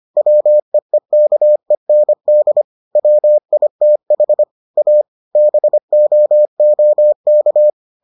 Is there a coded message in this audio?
Yes: Morse code from 0.2 to 7.7 s, heard as a keyed tone, about 25 wpm, 600 Hz. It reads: WEEKEND WITH A BOOK